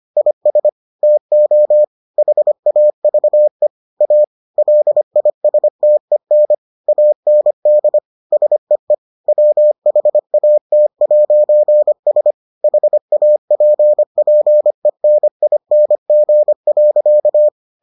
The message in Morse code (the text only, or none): IS TO HAVE A LISTEN AND SEE WHAT'S HAPPENING.